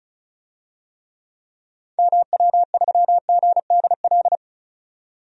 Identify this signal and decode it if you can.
Morse code, heard as a keyed tone, about 36 wpm, 700 Hz. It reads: MW3GDL
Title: DXCC-100 Call Signs - 6 Characters - 35wpm